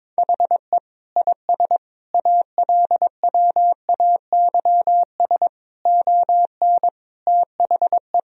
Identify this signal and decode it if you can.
Morse code, heard as a keyed tone, about 21 wpm, 700 Hz. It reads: HE IS ALWAYS ON THE
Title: Sentences from Top 1000 Words - Review Part 3 - 22wpm